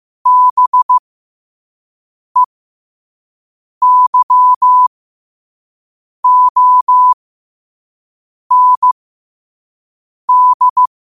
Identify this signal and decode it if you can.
Morse code, heard as a keyed tone, about 15 wpm, spaced as for 5 wpm, 1000 Hz. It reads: BEYOND
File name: qotdfiles / 5wpm